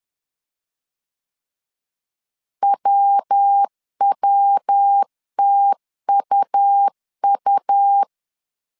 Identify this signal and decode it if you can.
Morse code, heard as a keyed tone, about 10 wpm, 800 Hz. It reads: WWTUU
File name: CW / DET / 1 3 1500C S B 10wpm 3335